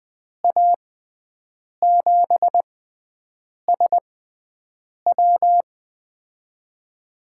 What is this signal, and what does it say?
Morse code, heard as a keyed tone, about 20 wpm, spaced as for 7 wpm, 700 Hz. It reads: A7SW
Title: USA Call Signs - 5 Characters - 6x Character Spacing - 20wpm